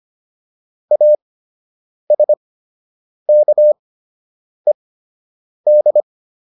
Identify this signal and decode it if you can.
Morse code, heard as a keyed tone, about 25 wpm, spaced as for 8 wpm, 600 Hz. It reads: ASKED